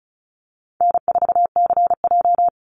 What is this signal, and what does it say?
Morse code, heard as a keyed tone, about 34 wpm, 700 Hz. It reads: N4CJ
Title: Easy Call Signs Repeated 3x - 35wpm